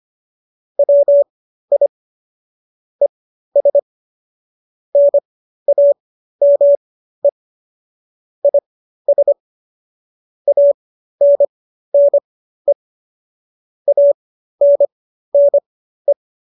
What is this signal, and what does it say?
Morse code, heard as a keyed tone, about 25 wpm, spaced as for 13 wpm, 570 Hz. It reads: WI ES NAME IS ANNE ANNE